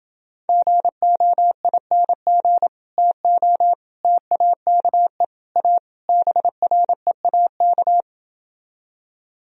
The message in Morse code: GOING TO TAKE A BREAK

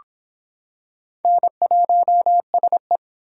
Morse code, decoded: N1SE

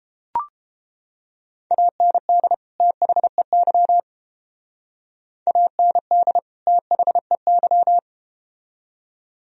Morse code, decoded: AND THEY AND THEY

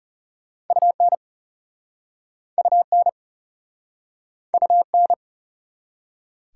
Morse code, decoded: UN UN UN

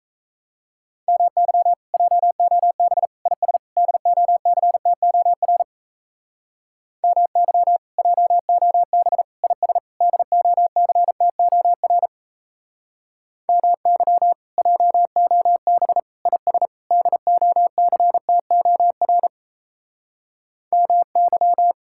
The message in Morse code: MY JOB IS DOCTOR MY JOB IS DOCTOR MY JOB IS DOCTOR MY

